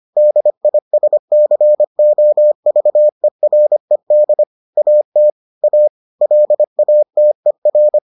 Morse code, heard as a keyed tone, about 25 wpm, 600 Hz. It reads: DISCOVERED AT A LATER